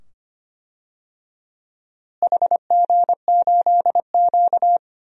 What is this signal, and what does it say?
Morse code, heard as a keyed tone, about 25 wpm, 700 Hz. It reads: HG8Q
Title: DXCC-100 Call Signs - 4 Characters - Repeated 2x - 25wpm